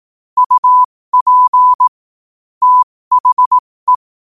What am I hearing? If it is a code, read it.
Morse code, heard as a keyed tone, about 18 wpm, spaced as for 15 wpm, 1000 Hz. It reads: UP THE
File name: qotdfiles / 15wpm